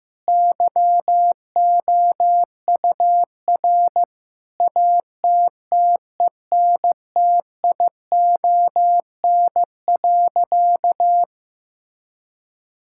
Morse code, heard as a keyed tone, about 15 wpm, 700 Hz. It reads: YOUR ATTENTION.
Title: Sentences from Top 1000 Words - Review Part 6 - 15wpm